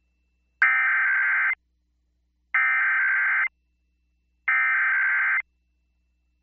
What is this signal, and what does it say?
Emergency Alert System (SAME) tones (header bursts), heard as harsh buzzing bursts. It is heard from 0.6 to 5.4 s.